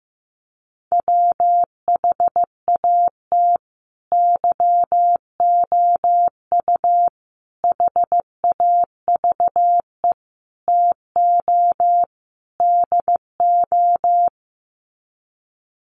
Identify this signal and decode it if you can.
Morse code, heard as a keyed tone, about 15 wpm, 700 Hz. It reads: WHAT YOU HAVE TO DO
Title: Sets of 5 Words Encore 15wpm